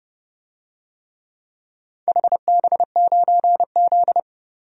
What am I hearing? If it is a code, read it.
Morse code, heard as a keyed tone, about 32 wpm, 700 Hz. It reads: HB9Z
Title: DXCC-100 Call Signs - 4 Characters - Repeated 3x - 30wpm